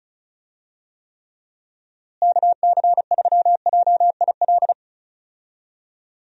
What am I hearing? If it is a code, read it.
Morse code, heard as a keyed tone, about 34 wpm, 700 Hz. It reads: KC3JIL
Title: USA Call Signs - 6 Characters - Repeated 3x - 35wpm